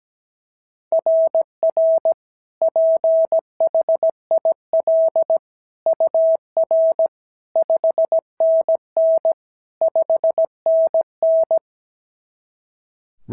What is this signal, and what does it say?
Morse code, heard as a keyed tone, about 16 wpm, 650 Hz. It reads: RR PHIL UR 5NN 5NN
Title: SOTA QSO Practice 17wpm